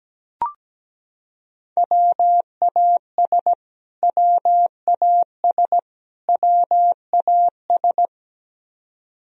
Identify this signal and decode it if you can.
Morse code, heard as a keyed tone, about 17 wpm, 700 Hz. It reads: WAS WAS WAS
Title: Top 100 Words Repeated 17wpm